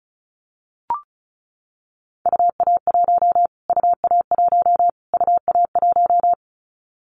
Morse code, heard as a keyed tone, about 34 wpm, 700 Hz. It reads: UA1 UA1 UA1
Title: Callsign Prefixes for Top 25 DXCC - Repeated 3x - 35wpm